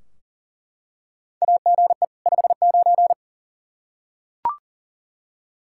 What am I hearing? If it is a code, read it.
Morse code, heard as a keyed tone, about 40 wpm, 700 Hz. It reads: AGE 59
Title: K1USN SST QSO Practice 40wpm